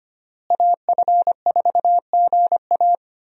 Morse code, heard as a keyed tone, about 25 wpm, 700 Hz. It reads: AF4GA
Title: USA Call Signs - 5 Characters - Repeated 2x - 25wpm